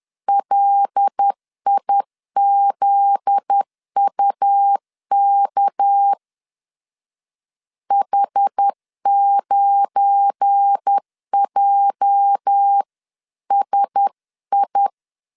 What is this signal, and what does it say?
Morse code, heard as a keyed tone, about 10 wpm, 800 Hz. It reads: LIZUK H9JSI
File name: LEZIONI / 8 YJ45 / 8 YJ 10wpm 2651